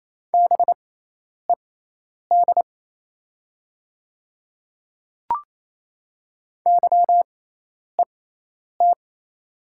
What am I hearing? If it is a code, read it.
Morse code, heard as a keyed tone, about 28 wpm, spaced as for 10 wpm, 700 Hz. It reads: BED YET
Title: 6x Character Spacing - Top 500 Words - 28wpm